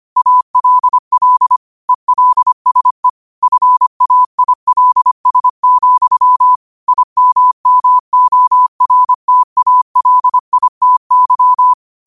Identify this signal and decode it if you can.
Morse code, heard as a keyed tone, about 25 wpm, 1000 Hz. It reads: ALL ELSE FAILS, IMMORTALITY
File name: qotdfiles / 25wpm